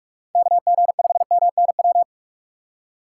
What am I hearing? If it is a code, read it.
Morse code, heard as a keyed tone, about 46 wpm, 700 Hz. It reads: KG5MNW
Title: USA Call Signs - 6 Characters - 3x Character Spacing - 45wpm